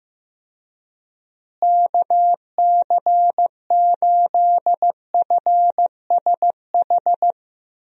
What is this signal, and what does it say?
Morse code, heard as a keyed tone, about 15 wpm, 700 Hz. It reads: KC8FSH
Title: USA Call Signs - 6 Characters - Repeated 2x - 15wpm